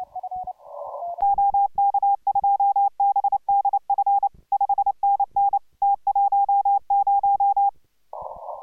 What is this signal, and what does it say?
Morse code, heard as a keyed tone, about 28 wpm, 810 Hz. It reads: OK2BDF 5NN T10